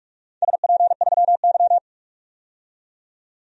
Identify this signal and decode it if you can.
Morse code, heard as a keyed tone, about 44 wpm, 700 Hz. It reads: SP3Y
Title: DXCC-100 Call Signs - 4 Characters - 45wpm